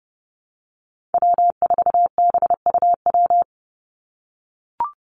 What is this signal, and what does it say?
Morse code, heard as a keyed tone, about 30 wpm, 700 Hz. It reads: W4BUW
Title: Call Signs - 30wpm